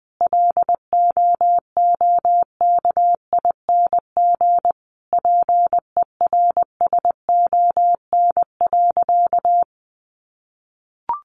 Morse code, heard as a keyed tone, about 20 wpm, 700 Hz. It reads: LOOKING PERSON.
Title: Easy Phrases 20wpm